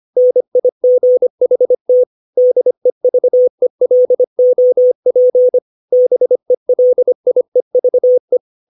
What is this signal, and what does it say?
Morse code, heard as a keyed tone, about 25 wpm, 500 Hz. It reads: NIGHT DEVELOP BELIEVE
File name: 632 words / 632 Words 25WPM